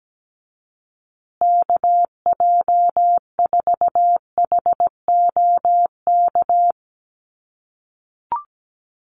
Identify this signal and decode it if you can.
Morse code, heard as a keyed tone, about 17 wpm, 700 Hz. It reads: KJ4HOK
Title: USA Call Signs Repeated 2x 17wpm